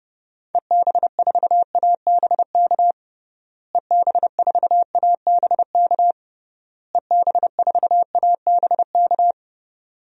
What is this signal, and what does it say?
Morse code, heard as a keyed tone, about 30 wpm, 700 Hz. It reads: EB4ABK EB4ABK EB4ABK
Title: DXCC-100 Call Signs - 6 Characters - Repeated 3x - 30wpm